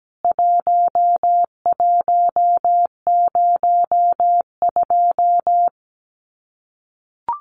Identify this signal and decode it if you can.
Morse code, heard as a keyed tone, about 17 wpm, 700 Hz. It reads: 1102
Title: CWT Exchange 17wpm